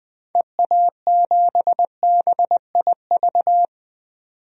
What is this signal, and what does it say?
Morse code, heard as a keyed tone, about 20 wpm, 700 Hz. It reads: EA7BIV
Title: DXCC-100 Call Signs - 6 Characters - 20wpm